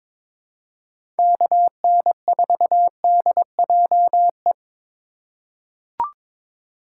Morse code, heard as a keyed tone, about 22 wpm, 700 Hz. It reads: KN4DJE
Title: USA Call Signs - 6 Characters - 3x Character Spacing - 22wpm